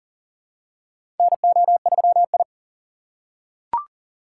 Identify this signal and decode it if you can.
Morse code, heard as a keyed tone, about 38 wpm, 700 Hz. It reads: NO3I
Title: USA Call Signs - 4 Characters - 2x Character Spacing - 40wpm